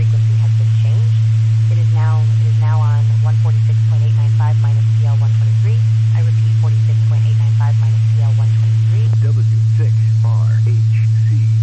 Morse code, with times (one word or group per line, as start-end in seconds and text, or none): none